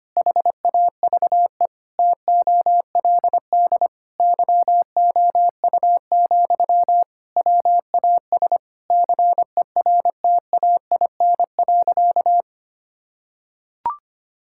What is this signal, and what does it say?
Morse code, heard as a keyed tone, about 25 wpm, 700 Hz. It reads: HAVE TOLD YOU, WAS CERTAIN.
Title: Sentences from Top 1000 Words - Review Part 6 - 25wpm